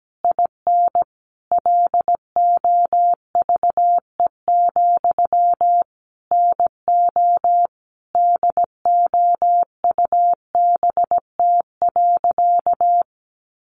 Morse code, heard as a keyed tone, about 17 wpm, 700 Hz. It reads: IN LOVE, NO DOUBT.
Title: Sentences from Top 1500 Words 17wpm